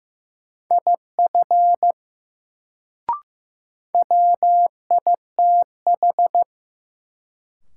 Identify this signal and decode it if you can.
Morse code, heard as a keyed tone, about 15 wpm, 700 Hz. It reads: IF WITH
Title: Sets of 2 Words - New Word Drill - 15wpm